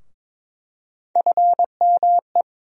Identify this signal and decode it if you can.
Morse code, heard as a keyed tone, about 22 wpm, 700 Hz. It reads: FME
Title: USA Call Sign Postfixes - 3 Letters 22wpm